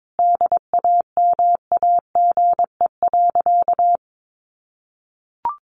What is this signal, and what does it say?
Morse code, heard as a keyed tone, about 22 wpm, 700 Hz. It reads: DAMAGE.
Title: Sentences from Top 1100 Words 22wpm